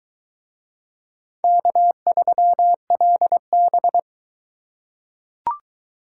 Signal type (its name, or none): Morse code